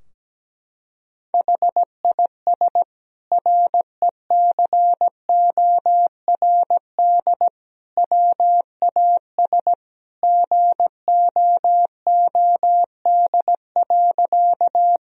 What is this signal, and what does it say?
Morse code, heard as a keyed tone, about 17 wpm, 700 Hz. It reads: HIS RECORD WAS GOOD.
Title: Sentences from Top 1000 Words - Review Part 3 - 17wpm